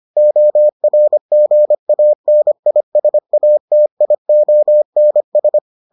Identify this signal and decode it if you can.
Morse code, heard as a keyed tone, about 25 wpm, 600 Hz. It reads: ORGANISATIONS